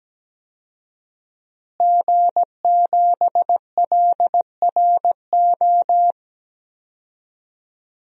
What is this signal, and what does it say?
Morse code, heard as a keyed tone, about 17 wpm, 700 Hz. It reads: G7LRO
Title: DXCC-100 Call Signs - 5 Characters - Repeated 3x - 17wpm